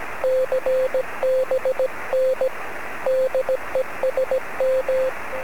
Morse code, heard as a keyed tone, about 17 wpm, 520 Hz, with radio noise. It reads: CBN DE SM
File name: sm7lcb / 17-sm0fzh-23cm